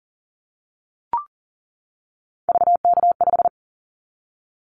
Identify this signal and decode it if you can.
Morse code, heard as a keyed tone, about 38 wpm, 700 Hz. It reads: VK5
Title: Callsign Prefixes for Top 50 DXCC - 40wpm